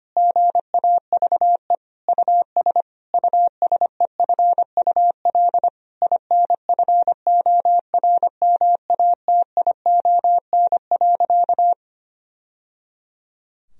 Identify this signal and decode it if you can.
Morse code, heard as a keyed tone, about 25 wpm, 700 Hz. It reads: GAVE US USEFUL INFORMATION.
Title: Sentences from Top 1000 Words - Review Part 4 - 25wpm